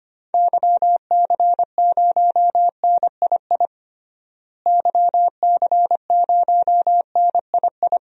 Morse code, heard as a keyed tone, about 25 wpm, 700 Hz. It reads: YC0NII YC0NII
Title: DXCC-100 Call Signs - 6 Characters - Repeated 2x - 25wpm